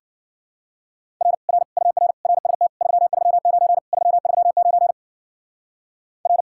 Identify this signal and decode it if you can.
Morse code, heard as a keyed tone, about 47 wpm, 700 Hz. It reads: R R UR RST 339 339 R
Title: Speed-Racing Common QSO Phrases - 40wpm